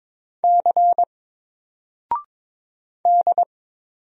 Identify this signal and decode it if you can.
Morse code, heard as a keyed tone, about 23 wpm, 700 Hz. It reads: C D
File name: Single-Letter-Number-22wpm